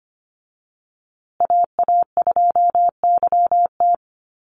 Morse code, heard as a keyed tone, about 25 wpm, 700 Hz. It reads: AA2YT
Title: USA Call Signs Repeated 3x 25wpm